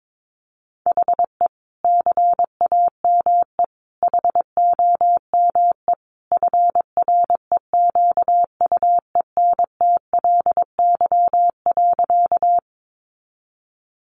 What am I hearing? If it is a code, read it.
Morse code, heard as a keyed tone, about 22 wpm, 700 Hz. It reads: HE CAME HOME FREQUENTLY.
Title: Sentences from Top 1300 Words 22wpm